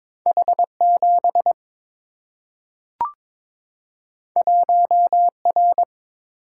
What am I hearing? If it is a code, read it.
Morse code, heard as a keyed tone, about 22 wpm, 700 Hz. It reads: H7 1R
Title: Single-Letter Single-Number 22wpm